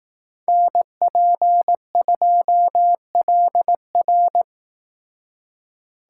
Morse code, heard as a keyed tone, about 19 wpm, 700 Hz. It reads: NP2LR